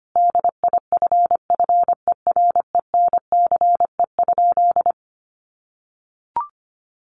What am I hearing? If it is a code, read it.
Morse code, heard as a keyed tone, about 25 wpm, 700 Hz. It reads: DIFFERENCE?
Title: Sentences from Top 800 Words 25wpm